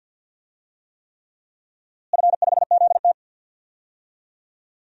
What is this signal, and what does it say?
Morse code, heard as a keyed tone, about 47 wpm, 700 Hz. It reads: F5ZT
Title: DXCC-100 Call Signs - 4 Characters - Repeated 3x - 50wpm